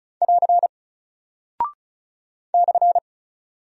Morse code, heard as a keyed tone, about 36 wpm, 700 Hz. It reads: <AR> /
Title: Single Character (Letter:Number:Punctuation:Prosign) Alt 35wpm